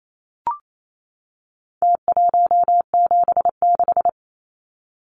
Morse code, heard as a keyed tone, about 28 wpm, 700 Hz. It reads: T176
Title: Numbers - Contest Style - 4-Digit 28wpm